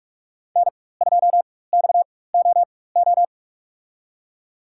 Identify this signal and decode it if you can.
Morse code, heard as a keyed tone, about 44 wpm, spaced as for 21 wpm, 700 Hz. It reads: N2XOO
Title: USA Call Signs - 5 Characters - 4x Character Spacing - 45wpm